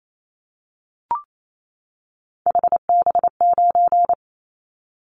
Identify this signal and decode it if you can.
Morse code, heard as a keyed tone, about 28 wpm, 700 Hz. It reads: HB9
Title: Callsign Prefixes for Top 25 DXCC - 28wpm